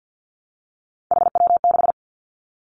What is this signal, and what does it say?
Morse code, heard as a keyed tone, about 50 wpm, 700 Hz. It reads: HW6
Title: Three Challenging Letter-Number Combinations 50wpm